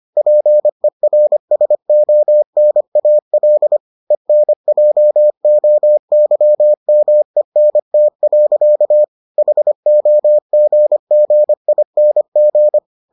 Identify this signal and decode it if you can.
Morse code, heard as a keyed tone, about 25 wpm, 600 Hz. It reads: PERSONAL ENJOYMENT. HOGGING